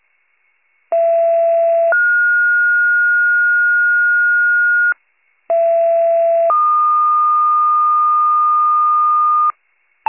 Two-tone page, heard as a one-second tone, then a three-second tone, about 670 then 1430 Hz, about 670 then 1220 Hz, with noise. It is heard from 0.9 to 9.5 s.